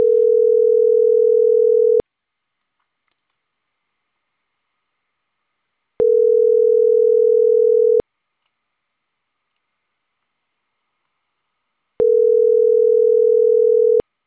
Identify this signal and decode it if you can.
Ringback tone (North America), 440 + 480 Hz.